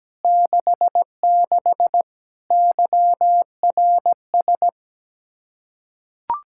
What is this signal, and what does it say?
Morse code, heard as a keyed tone, about 17 wpm, 700 Hz. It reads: 66 YRS
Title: Practice QSO - Exchange 3 - More Info - 17wpm